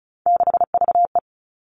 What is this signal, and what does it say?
Morse code, heard as a keyed tone, about 36 wpm, 700 Hz. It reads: 6VE
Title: Three Challenging Letter-Number Combinations 35wpm